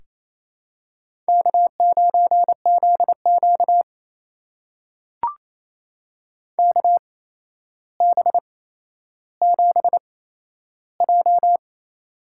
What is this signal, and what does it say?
Morse code, heard as a keyed tone, about 28 wpm, spaced as for 12 wpm, 700 Hz. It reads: K9ZQ KB7J
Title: USA Call Signs - 8x Character Spacing - 28wpm